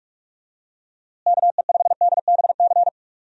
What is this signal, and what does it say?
Morse code, heard as a keyed tone, about 43 wpm, 700 Hz. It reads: KE5DBC
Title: USA Call Signs - 6 Characters - Repeated 2x - 45wpm